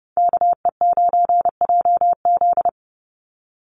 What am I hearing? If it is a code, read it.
Morse code, heard as a keyed tone, about 29 wpm, 700 Hz. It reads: KE9JZ
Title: USA Call Signs 30wpm